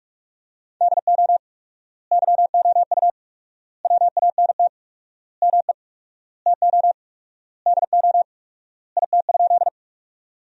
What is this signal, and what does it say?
Morse code, heard as a keyed tone, about 47 wpm, 700 Hz. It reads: DO YOU WANT ME TO DO IT?
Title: 4x Word-Spacing - Sentences from Top 200 Words - 45wpm